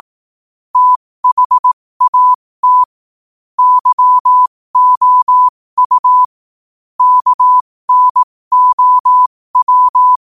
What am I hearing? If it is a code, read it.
Morse code, heard as a keyed tone, about 18 wpm, spaced as for 15 wpm, 1000 Hz. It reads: THAT YOU KNOW
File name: qotdfiles / 15wpm